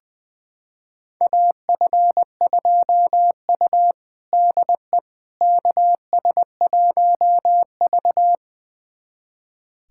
Morse code, heard as a keyed tone, about 21 wpm, 700 Hz. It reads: AF2U DE KS1V
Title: USA Callsign DE Callsign - 4 Characters - 20wpm